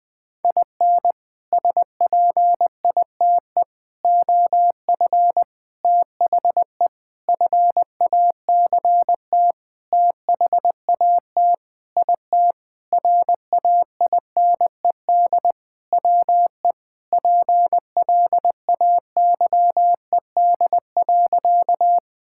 Morse code, heard as a keyed tone, about 20 wpm, 700 Hz. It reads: IN SPITE OF THE FACT THAT IT RAINED WE PLAYED.
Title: Sentences with 6-Word Phrases - Spoken Beginning - 20wpm